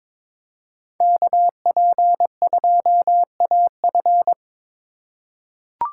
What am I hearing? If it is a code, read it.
Morse code, heard as a keyed tone, about 21 wpm, 700 Hz. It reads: KP2AF
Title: USA Call Signs - 5 Characters - 2x Character Spacing - 22wpm